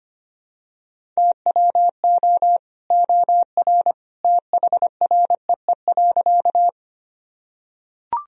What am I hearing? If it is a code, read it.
Morse code, heard as a keyed tone, about 25 wpm, 700 Hz. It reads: TWO OR THREE.
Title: Sentences from Top 1000 Words - Review Part 1 - 25wpm